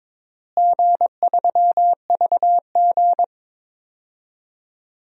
Morse code, heard as a keyed tone, about 22 wpm, 700 Hz. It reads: G3VG
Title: DXCC-100 Call Signs - 4 Characters - Repeated 3x - 22wpm